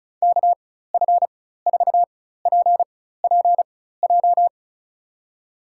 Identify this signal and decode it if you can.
Morse code, heard as a keyed tone, about 36 wpm, spaced as for 16 wpm, 690 Hz. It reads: KF4PPJ
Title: USA Call Signs - 6 Characters - 4x Character Spacing - 35wpm